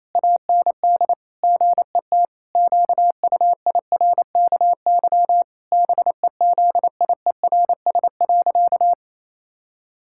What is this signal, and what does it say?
Morse code, heard as a keyed tone, about 28 wpm, 700 Hz. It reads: AND GET QUIRKY BEZIERS.
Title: Pangram Challenge v2 - 28wpm